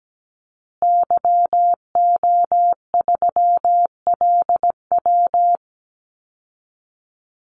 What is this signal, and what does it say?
Morse code, heard as a keyed tone, about 17 wpm, 700 Hz. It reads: YO3LW
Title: Call Signs Repeated 2x - 17wpm